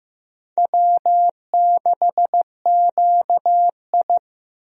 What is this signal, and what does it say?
Morse code, heard as a keyed tone, about 15 wpm, 700 Hz. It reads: W6QI
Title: USA Call Signs - 4 Characters - Repeated 3x - 15wpm